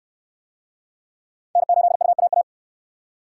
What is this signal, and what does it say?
Morse code, heard as a keyed tone, about 68 wpm, 700 Hz. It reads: N8UAA